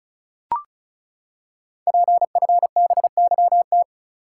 Morse code, heard as a keyed tone, about 36 wpm, 690 Hz. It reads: PFBYT
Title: Code Groups - 5 Letters - 35wpm